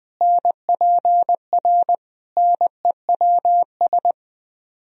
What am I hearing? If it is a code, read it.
Morse code, heard as a keyed tone, about 20 wpm, 700 Hz. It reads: NPR NEWS
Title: Speed-Racing - Sets of 5 Words 17wpm